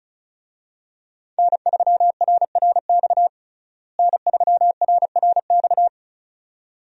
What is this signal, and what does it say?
Morse code, heard as a keyed tone, about 36 wpm, 700 Hz. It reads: N3RRX N3RRX
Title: USA Call Signs - 5 Characters - Repeated 2x - 35wpm